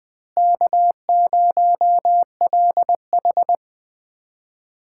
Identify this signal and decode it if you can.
Morse code, heard as a keyed tone, about 20 wpm, 700 Hz. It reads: K0LH
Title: USA Callsign DE Callsign - 4 Characters - 20wpm